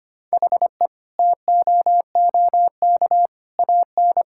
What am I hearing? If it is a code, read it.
Morse code, heard as a keyed tone, about 25 wpm, 700 Hz. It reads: HE TOOK AN
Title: Sentences from Top 1000 Words - Review Part 6 - 25wpm